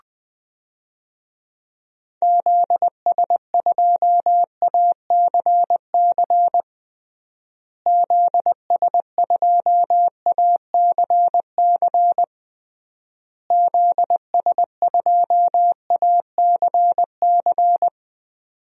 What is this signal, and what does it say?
Morse code, heard as a keyed tone, about 20 wpm, 700 Hz. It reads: ZS2ACC ZS2ACC ZS2ACC
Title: DXCC-100 Call Signs - 6 Characters - Repeated 3x - 20wpm